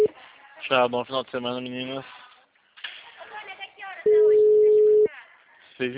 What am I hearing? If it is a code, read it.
Ringback tone (Europe, 425 Hz), with speech.